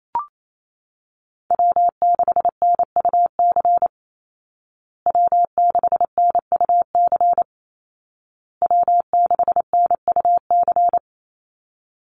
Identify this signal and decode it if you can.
Morse code, heard as a keyed tone, about 28 wpm, 700 Hz. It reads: W6NUC W6NUC W6NUC
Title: Call Signs Repeated 2x - 28wpm